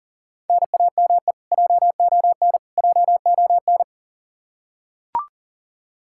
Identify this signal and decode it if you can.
Morse code, heard as a keyed tone, about 40 wpm, 700 Hz. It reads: NAME JON JON
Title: K1USN SST QSO Practice 40wpm